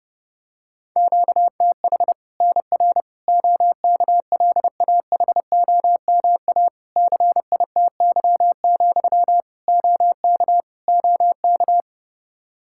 Morse code, heard as a keyed tone, about 30 wpm, 700 Hz. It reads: QTH NR OKLAHOMA CITY, OK OK
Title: Practice QSO - Exchange 1 - Beginning - 30wpm